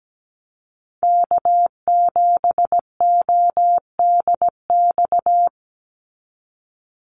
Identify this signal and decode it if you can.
Morse code, heard as a keyed tone, about 17 wpm, 700 Hz. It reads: K7ODX
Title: NA SOTA Activator Callsigns 17wpm